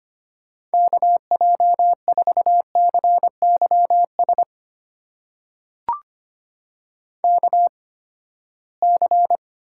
Morse code, heard as a keyed tone, about 25 wpm, spaced as for 17 wpm, 700 Hz. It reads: KJ4CYS KC
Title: USA Call Signs - 6 Characters - 8x Character Spacing - 25wpm